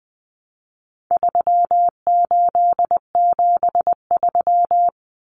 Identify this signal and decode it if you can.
Morse code, heard as a keyed tone, about 20 wpm, 700 Hz. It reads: 3873
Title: Numbers - 4-Digit 20wpm